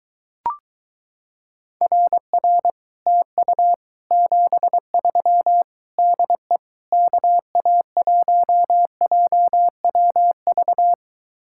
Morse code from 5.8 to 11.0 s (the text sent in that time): DE KA1JWV